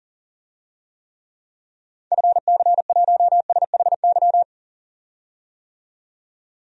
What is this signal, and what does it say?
Morse code, heard as a keyed tone, about 40 wpm, 700 Hz. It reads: FC1SHY
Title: DXCC-100 Call Signs - 6 Characters - 40wpm